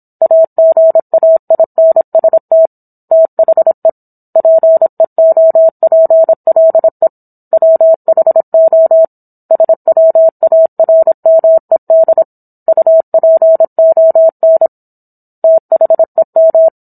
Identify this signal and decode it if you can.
Morse code, heard as a keyed tone, about 27 wpm, 650 Hz. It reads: AGAINST THE PEOPLE WHO SWARMED UPON THEM
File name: Book One / 17-Chapter 17 - 26 WPM